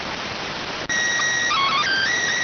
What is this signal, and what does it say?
SSTV (slow-scan television), PD 290.